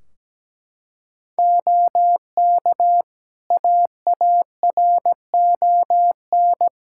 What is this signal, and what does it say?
Morse code, heard as a keyed tone, about 17 wpm, 700 Hz. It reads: OK AARON
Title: K1USN SST QSO Practice 17wpm